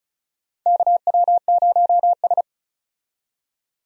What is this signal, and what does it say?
Morse code, heard as a keyed tone, about 36 wpm, 700 Hz. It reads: KW0S
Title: USA Call Signs - 4 Characters - 35wpm